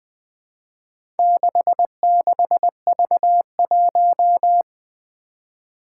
Morse code, heard as a keyed tone, about 20 wpm, 700 Hz. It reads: 66V1
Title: Four Challenging Letter-Number Combinations 20wpm